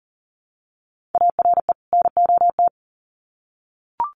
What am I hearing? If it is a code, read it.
Morse code, heard as a keyed tone, about 40 wpm, 700 Hz. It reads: ARE NOT
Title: Sets of 2 Words 40wpm